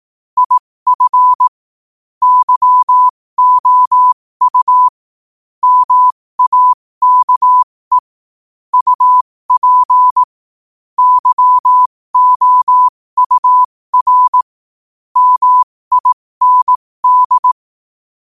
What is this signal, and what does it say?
Morse code, heard as a keyed tone, about 18 wpm, spaced as for 15 wpm, 1000 Hz. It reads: IF YOU MAKE UP YOUR MIND
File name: qotdfiles / 15wpm